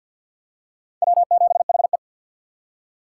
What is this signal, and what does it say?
Morse code, heard as a keyed tone, about 47 wpm, 700 Hz. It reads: W7HE
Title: USA Call Signs - 4 Characters - 2x Character Spacing - 50wpm